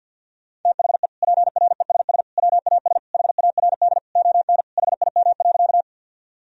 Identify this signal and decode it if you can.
Morse code, heard as a keyed tone, about 46 wpm, 700 Hz. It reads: THE PRESS WAS HARD ON HIM.
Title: Sentences from Top 1000 Words - Review Part 1 - 50wpm